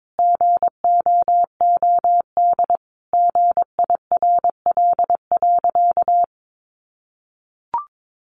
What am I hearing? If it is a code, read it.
Morse code, heard as a keyed tone, about 22 wpm, 700 Hz. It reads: GOOD GIRL.
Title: Sentences from Top 400 Words 22wpm